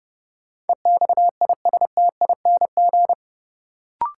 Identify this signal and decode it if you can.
Morse code, heard as a keyed tone, about 32 wpm, 700 Hz. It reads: EXISTING
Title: Sentences from Top 1500 Words - New Word Drill - 30wpm